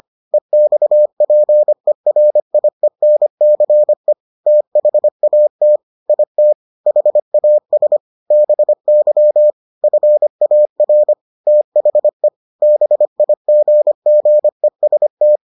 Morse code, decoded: EXPERIENCE THAT IT HAS BY FAR THE BIGGEST

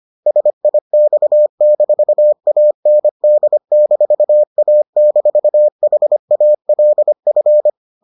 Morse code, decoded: SIX-AND-A-HALF

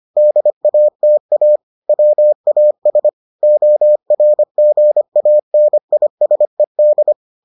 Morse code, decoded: DATA WAS ORGANISED

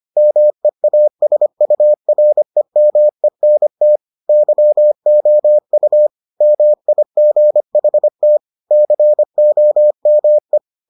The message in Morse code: MEASUREMENT YOU MIGHT COME

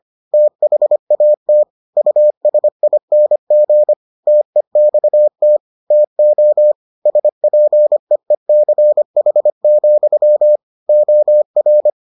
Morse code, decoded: THAT USING TEXT TO SPEECH, OR

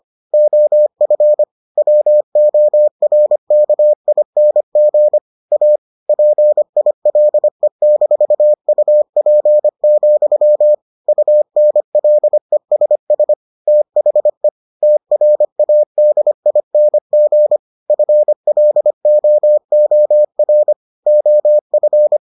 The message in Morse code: OF WORKING A PILE-UP, UNLESS THE TRADING FLOOR OF

